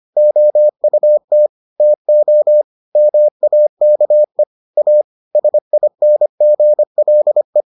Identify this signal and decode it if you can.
Morse code, heard as a keyed tone, about 25 wpm, 600 Hz. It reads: OUT TO MAKE A SINGLE